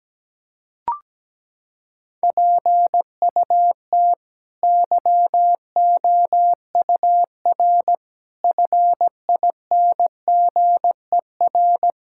Morse code, heard as a keyed tone, about 17 wpm, 700 Hz. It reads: PUT YOUR FINGER
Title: Sentences from Top 1600 Words 17wpm